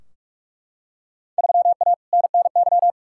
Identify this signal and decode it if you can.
Morse code, heard as a keyed tone, about 44 wpm, 700 Hz. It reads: 3A NNY
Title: Field Day Exchanges - 45wpm